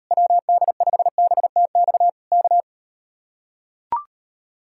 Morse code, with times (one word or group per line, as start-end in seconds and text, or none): 0.1-2.1 WD5BTX
2.3-2.6 K